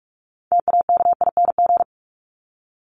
Morse code, heard as a keyed tone, about 44 wpm, 700 Hz. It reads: TAKING
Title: ING Suffix - Top 100 Words with Base Words 45wpm